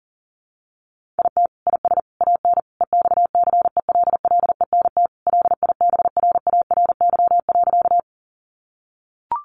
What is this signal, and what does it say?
Morse code, heard as a keyed tone, about 40 wpm, 700 Hz. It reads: IT IS AN EXCELLENT LIBRARY.
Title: Sentences from Top 1000 Words 40wpm